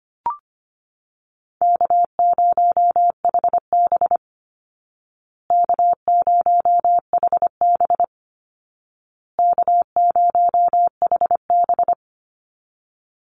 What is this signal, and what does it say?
Morse code, heard as a keyed tone, about 25 wpm, 700 Hz. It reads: K0HB K0HB K0HB
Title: Easy Call Signs Repeated 3x - 25wpm